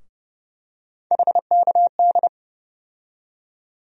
Morse code, heard as a keyed tone, about 32 wpm, 700 Hz. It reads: HKD
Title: USA Call Sign Postfixes - 3 Letters 30wpm